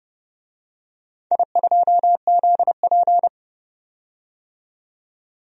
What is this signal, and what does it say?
Morse code, heard as a keyed tone, about 30 wpm, 700 Hz. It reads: I2ZP
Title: DXCC-100 Call Signs - 4 Characters - 30wpm